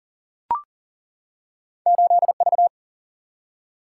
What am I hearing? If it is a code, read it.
Morse code, heard as a keyed tone, about 42 wpm, 700 Hz. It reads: 8V
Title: Single-Letter Single-Number 40wpm